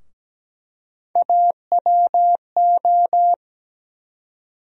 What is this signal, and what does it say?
Morse code, heard as a keyed tone, about 17 wpm, 700 Hz. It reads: AWO
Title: USA Call Sign Postfixes - 3 Letters 17wpm